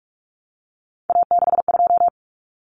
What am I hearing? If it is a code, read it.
Morse code, heard as a keyed tone, about 44 wpm, 700 Hz. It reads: A62
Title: Common Cut Numbers - 3-Digit 45wpm